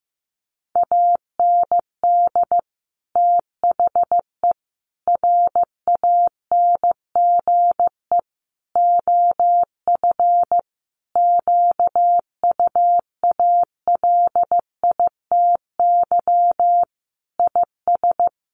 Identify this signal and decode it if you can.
Morse code, heard as a keyed tone, about 14 wpm, 700 Hz. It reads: AND THE RANGE OF QUALITY IS
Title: Sentences from Top 1500 Words 15wpm